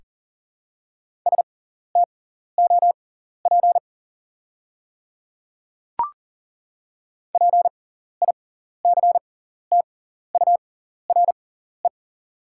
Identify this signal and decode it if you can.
Morse code, heard as a keyed tone, about 39 wpm, spaced as for 14 wpm, 700 Hz. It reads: STOP PICTURE
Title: 6x Character Spacing - Top 300 Words - 40wpm